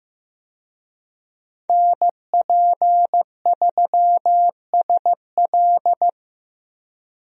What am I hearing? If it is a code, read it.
Morse code, heard as a keyed tone, about 15 wpm, 700 Hz. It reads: NP3SL
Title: USA Call Signs - 5 Characters - 15wpm